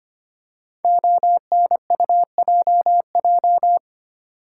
Morse code, heard as a keyed tone, about 25 wpm, 700 Hz. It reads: ONUJJ
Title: Code Groups - 5 Letters - 25wpm